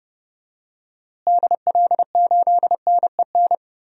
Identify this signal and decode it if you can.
Morse code, heard as a keyed tone, about 30 wpm, 700 Hz. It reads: DL8NEN
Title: DXCC-100 Call Signs - 6 Characters - Repeated 3x - 30wpm